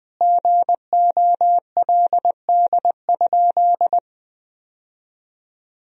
Morse code, heard as a keyed tone, about 21 wpm, 700 Hz. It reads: GOLD?
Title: Sentences from Top 1000 Words - Review Part 3 - 20wpm